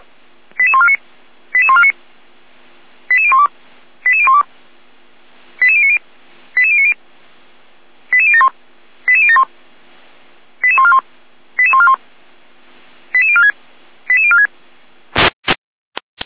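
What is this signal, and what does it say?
Five-tone selective call (ZVEI), heard as quick runs of five tones, tones of about 70 ms. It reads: 88149 88149 88012 88012 88090 88090 88071 88071 88141 88141 88046 88046
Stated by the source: ZVEI five-tone call received on an SDR.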